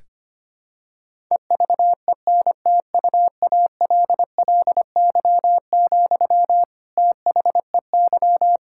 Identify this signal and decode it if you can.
Morse code, heard as a keyed tone, about 25 wpm, 700 Hz. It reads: EVENTUALLY, THEY
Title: Sentences from Top 1000 Words - Review Part 6 - 25wpm